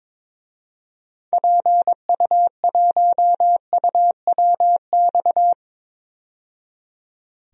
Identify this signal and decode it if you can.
Morse code, heard as a keyed tone, about 22 wpm, 700 Hz. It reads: PU1UWX
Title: DXCC-100 Call Signs - 6 Characters - 22wpm